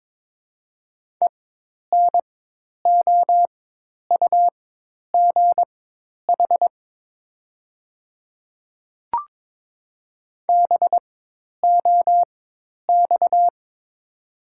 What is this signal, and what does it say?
Morse code, heard as a keyed tone, about 22 wpm, spaced as for 10 wpm, 700 Hz. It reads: ENOUGH BOX
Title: 4x Character Spacing - Top 500 Words - 22wpm